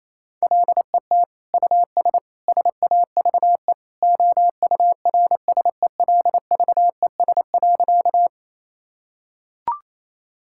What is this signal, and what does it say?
Morse code, heard as a keyed tone, about 28 wpm, 700 Hz. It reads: LET US SAVE OURSELVES.
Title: Sentences from Top 1600 Words 28wpm